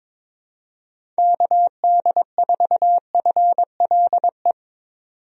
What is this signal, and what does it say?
Morse code, heard as a keyed tone, about 22 wpm, 700 Hz. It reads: KD4FLE
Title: USA Call Signs - 6 Characters - 22wpm